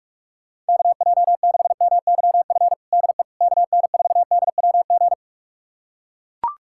Morse code, heard as a keyed tone, about 42 wpm, 700 Hz. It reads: KJ6MYF DE KN4DWG
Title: USA Callsign DE Callsign 45wpm